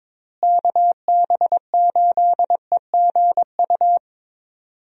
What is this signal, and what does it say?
Morse code, heard as a keyed tone, about 21 wpm, 700 Hz. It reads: KB8EGU
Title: USA Call Signs - 6 Characters - Repeated 3x - 22wpm